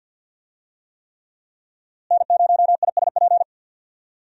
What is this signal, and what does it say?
Morse code, heard as a keyed tone, about 51 wpm, 700 Hz. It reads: N0ISP
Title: USA Call Signs - 5 Characters - Repeated 3x - 50wpm